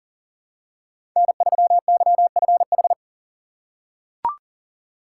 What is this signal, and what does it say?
Morse code, heard as a keyed tone, about 38 wpm, 700 Hz. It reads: N3YFH
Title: USA Call Signs - 5 Characters - 3x Character Spacing - 40wpm